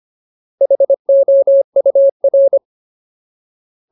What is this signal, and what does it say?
Morse code, heard as a keyed tone, about 25 wpm, 550 Hz. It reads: HOUR
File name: Ultimate Word List / 25WPM Ultimate